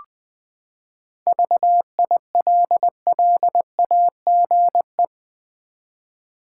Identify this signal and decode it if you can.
Morse code, heard as a keyed tone, about 21 wpm, 700 Hz. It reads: VILLAGE
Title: Seven Letter Words - 20wpm